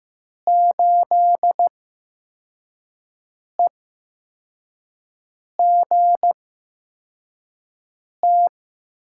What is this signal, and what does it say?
Morse code, heard as a keyed tone, about 15 wpm, spaced as for 4 wpm, 700 Hz. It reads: 8EGT
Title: USA Call Signs - 6 Characters - 8x Character Spacing - 15wpm